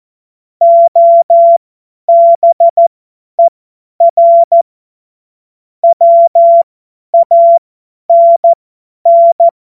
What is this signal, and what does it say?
Morse code, heard as a keyed tone, about 14 wpm, spaced as for 10 wpm, 680 Hz. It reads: OBER WANN
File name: Morsekurs / Witze / Witz 25. Mai 2021, 50 F70